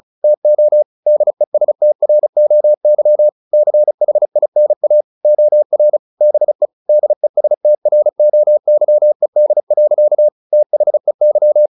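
Morse code, heard as a keyed tone, about 34 wpm, 600 Hz. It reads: TO DESTROY CHINA OR BE DESTROYED. THEY
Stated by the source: Morse practice at 35 wpm from CW med Gnister, sending Ken Follett's novel Never in English.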